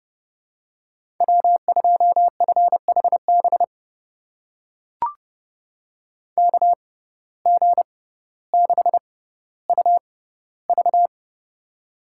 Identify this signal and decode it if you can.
Morse code, heard as a keyed tone, about 30 wpm, spaced as for 15 wpm, 700 Hz. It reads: W2FHB KG6UV
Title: USA Call Signs - 5 Characters - 6x Character Spacing - 30wpm